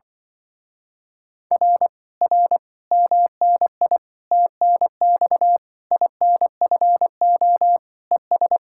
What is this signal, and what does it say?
Morse code, heard as a keyed tone, about 24 wpm, 700 Hz. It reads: R R MNI TNX INFO ES
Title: Speed-Racing Common QSO Phrases - 20wpm